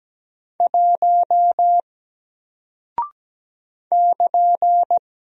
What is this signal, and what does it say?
Morse code, heard as a keyed tone, about 17 wpm, 700 Hz. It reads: 1 <KN>
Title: Single Character (Letter:Number:Punctuation:Prosign) 17wpm